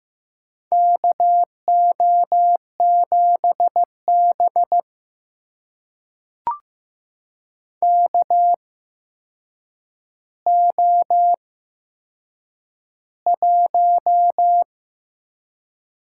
Morse code, heard as a keyed tone, about 15 wpm, spaced as for 7 wpm, 700 Hz. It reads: KO7B KO1
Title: USA Call Signs - 4 Characters - 8x Character Spacing - 15wpm